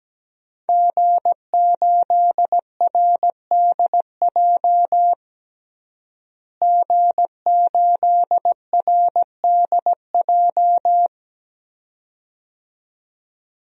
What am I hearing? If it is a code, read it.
Morse code, heard as a keyed tone, about 17 wpm, 700 Hz. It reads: G8RDJ G8RDJ
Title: DXCC-100 Call Signs - 5 Characters - Repeated 2x - 17wpm